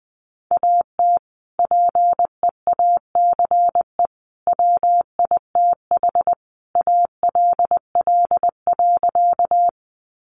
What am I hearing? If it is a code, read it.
Morse code, heard as a keyed tone, about 19 wpm, 700 Hz. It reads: AT PEACE WITH ALL.
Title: Sentences from Top 1500 Words 20wpm